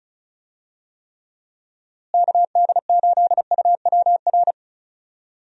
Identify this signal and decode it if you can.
Morse code, heard as a keyed tone, about 34 wpm, 700 Hz. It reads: KD8UWR
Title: USA Call Signs - 6 Characters - Repeated 2x - 35wpm